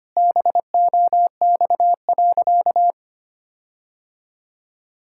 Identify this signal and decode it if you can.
Morse code, heard as a keyed tone, about 25 wpm, 700 Hz. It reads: BOX.
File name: practice / 25wpm / Sentences with 4-Word Phrases - 25wpm